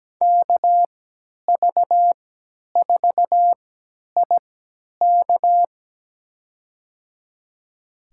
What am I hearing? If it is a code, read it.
Morse code, heard as a keyed tone, about 17 wpm, spaced as for 10 wpm, 700 Hz. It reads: KV4IK
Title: USA Call Signs - 5 Characters - 3x Character Spacing - 17wpm